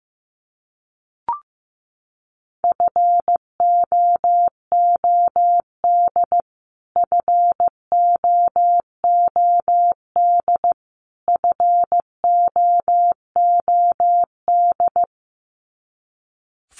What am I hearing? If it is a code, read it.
Morse code, heard as a keyed tone, about 15 wpm, 700 Hz. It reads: FOOD FOOD FOOD
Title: Top 500 Words Repeated 15wpm